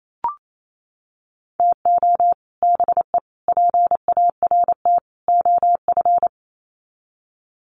Morse code, heard as a keyed tone, about 28 wpm, 700 Hz. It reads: TO BE PART OF
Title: Sets of 4 Words 28wpm